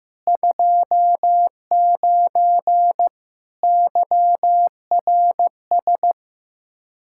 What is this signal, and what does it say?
Morse code, heard as a keyed tone, about 15 wpm, 700 Hz. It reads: 29 YRS
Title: K1USN SST QSO Practice 15wpm